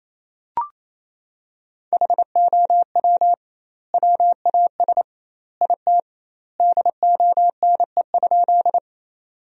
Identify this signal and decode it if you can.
Morse code, heard as a keyed tone, about 28 wpm, 700 Hz. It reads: HOW WAS IT DONE?
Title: 2x Word-Spacing - Sentences from Top 500 Words - 28wpm